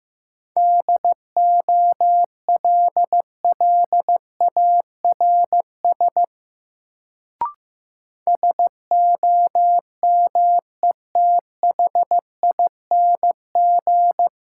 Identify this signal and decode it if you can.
Morse code, heard as a keyed tone, about 15 wpm, 700 Hz. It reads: DOLLARS SOMETHING
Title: Sets of 5 Words - New Word Drill - 15wpm